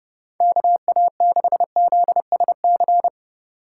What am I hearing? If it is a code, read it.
Morse code, heard as a keyed tone, about 30 wpm, 700 Hz. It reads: KA6ZSC
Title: USA Call Signs - 6 Characters - 2x Character Spacing - 30wpm